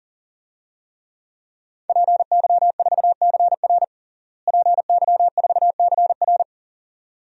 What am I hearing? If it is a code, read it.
Morse code, heard as a keyed tone, about 40 wpm, 700 Hz. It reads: PY4CR PY4CR
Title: DXCC-100 Call Signs - 5 Characters - Repeated 2x - 40wpm